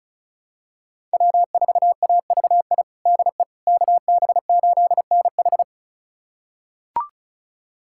Morse code, heard as a keyed tone, about 36 wpm, 700 Hz. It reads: W4AVI DE KB8NH
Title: USA Callsign DE Callsign - 5 Characters - 35wpm